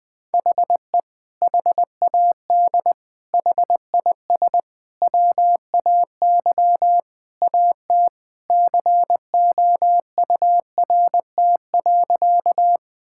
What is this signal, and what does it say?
Morse code, heard as a keyed tone, about 20 wpm, 700 Hz. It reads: HE HAD HIS WAY AT COURT.
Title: Sentences from Top 1600 Words 20wpm